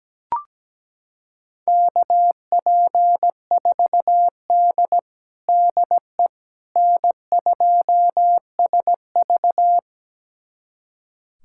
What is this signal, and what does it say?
Morse code, heard as a keyed tone, about 17 wpm, 700 Hz. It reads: KP4D DE N2SV
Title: USA Callsign DE Callsign - 4 Characters - 17wpm